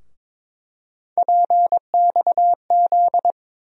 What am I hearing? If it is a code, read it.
Morse code, heard as a keyed tone, about 22 wpm, 700 Hz. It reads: PXZ
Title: USA Call Sign Postfixes - 3 Letters 22wpm